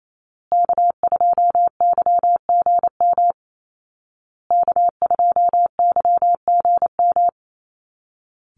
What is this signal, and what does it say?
Morse code, heard as a keyed tone, about 28 wpm, 700 Hz. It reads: K2YGM K2YGM
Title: Call Signs Repeated 2x - 28wpm